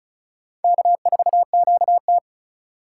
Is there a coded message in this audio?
Yes: Morse code from 0.6 to 2.2 s, heard as a keyed tone, about 45 wpm, 700 Hz. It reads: K4QT